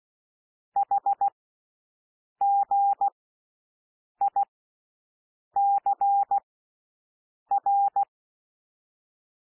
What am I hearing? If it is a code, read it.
Morse code, heard as a keyed tone, about 16 wpm, spaced as for 6 wpm, 800 Hz. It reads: HGICR
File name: cwmp3 / L1 SOUND LETTERS 16 wpm